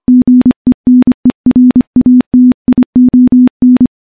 Morse code, heard as a keyed tone, about 25 wpm, 250 Hz. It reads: GENERATION
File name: morse / morse-training-25WPM